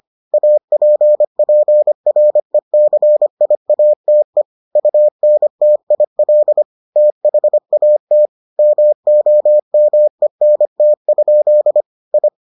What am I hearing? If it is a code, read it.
Morse code, heard as a keyed tone, about 25 wpm, 600 Hz. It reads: APPRECIATE UNTIL THAT MOMENT? I